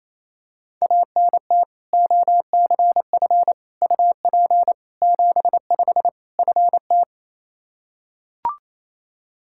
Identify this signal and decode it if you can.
Morse code, heard as a keyed tone, about 27 wpm, 700 Hz. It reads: ANT OCF UP 75 FT
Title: Practice QSO - Exchange 2 - Info - 28wpm